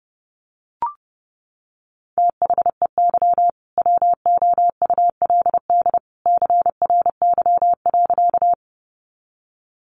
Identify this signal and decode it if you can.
Morse code, heard as a keyed tone, about 30 wpm, 700 Hz. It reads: THEY WOULD CRY.
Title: Sentences from Top 500 Words 30wpm